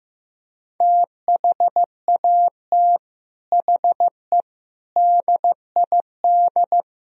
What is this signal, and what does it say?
Morse code, heard as a keyed tone, about 17 wpm, 700 Hz. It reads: THAT HE DID
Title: Speed-Racing - Sets of 4 Words 15wpm